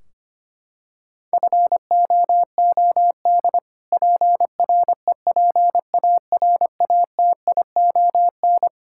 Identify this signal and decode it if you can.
Morse code, heard as a keyed tone, about 25 wpm, 700 Hz. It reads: FOOD PREPARATION
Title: K1USN SST QSO Practice 25wpm